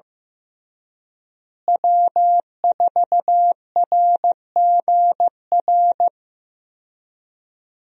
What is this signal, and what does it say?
Morse code, heard as a keyed tone, about 16 wpm, 700 Hz. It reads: W4RGR
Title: USA Call Signs - 5 Characters - Repeated 3x - 15wpm